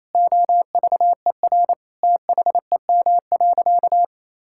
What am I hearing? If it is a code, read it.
Morse code, heard as a keyed tone, about 28 wpm, 700 Hz. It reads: OVER THEM.
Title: Sentences from Top 1000 Words - Review Part 6 - 28wpm